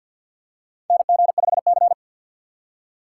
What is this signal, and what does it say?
Morse code, heard as a keyed tone, about 52 wpm, 700 Hz. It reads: NG5C
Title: USA Call Signs - 4x Character Spacing - 50wpm